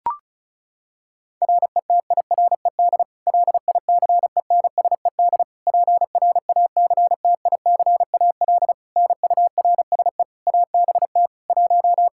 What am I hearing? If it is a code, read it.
Morse code, heard as a keyed tone, about 36 wpm, 700 Hz. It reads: RETIRED LICENSED PRACTICAL NURSE ABT 1
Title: Practice QSO - Exchange 3 - More Info - 35wpm